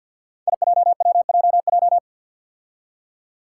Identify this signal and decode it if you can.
Morse code, heard as a keyed tone, about 46 wpm, 700 Hz. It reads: IJWJJ
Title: Five Challenging Letter-Number Combinations 50wpm